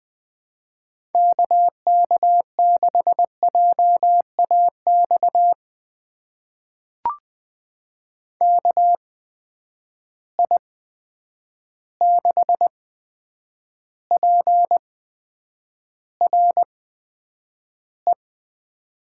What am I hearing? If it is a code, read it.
Morse code, heard as a keyed tone, about 20 wpm, spaced as for 9 wpm, 700 Hz. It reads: KK6JAX KI6PRE
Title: USA Call Signs - 6 Characters - 8x Character Spacing - 20wpm